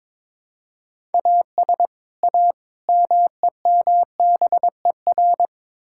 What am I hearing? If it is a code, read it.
Morse code, heard as a keyed tone, about 22 wpm, 700 Hz. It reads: AS A MEMBER
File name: practice / 22wpm / Build A Phrase - 5-Word Phrases - 22wpm